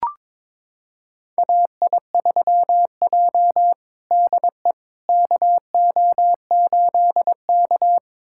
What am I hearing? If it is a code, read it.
Morse code, heard as a keyed tone, about 22 wpm, 700 Hz. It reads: AI3J DE KO8K
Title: USA Callsign DE Callsign - 4 Characters - 22wpm